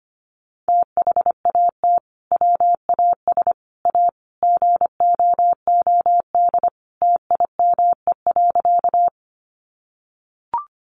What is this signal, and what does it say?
Morse code, heard as a keyed tone, about 25 wpm, 700 Hz. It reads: THAT WAS A GOOD TIME.
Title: Sentences from Top 200 Words 25wpm